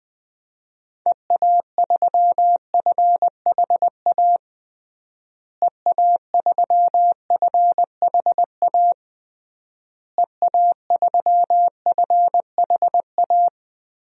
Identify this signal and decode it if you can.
Morse code, heard as a keyed tone, about 20 wpm, 700 Hz. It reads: EA3FHA EA3FHA EA3FHA
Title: DXCC-100 Call Signs - 6 Characters - Repeated 3x - 20wpm